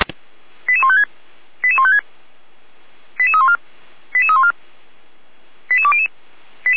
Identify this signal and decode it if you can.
Five-tone selective call (ZVEI), heard as quick runs of five tones, tones of about 70 ms. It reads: 88156 88156 88314 88314 88200